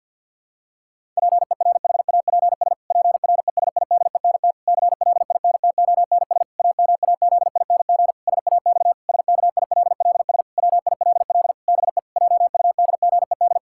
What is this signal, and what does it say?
Morse code, heard as a keyed tone, about 50 wpm, 700 Hz. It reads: PERHAPS PRESIDENT CLINTONS AMAZING SAX SKILLS WILL BE JUDGED
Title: Pangram Challenge v2 - 50wpm